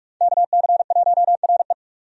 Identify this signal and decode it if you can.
Morse code, heard as a keyed tone, about 43 wpm, 700 Hz. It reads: KC1RE
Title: USA Call Signs - 5 Characters - 45wpm